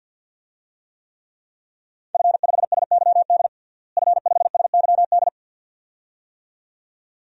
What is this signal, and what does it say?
Morse code, heard as a keyed tone, about 52 wpm, 700 Hz. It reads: F5SYD F5SYD
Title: DXCC-100 Call Signs - 5 Characters - Repeated 2x - 50wpm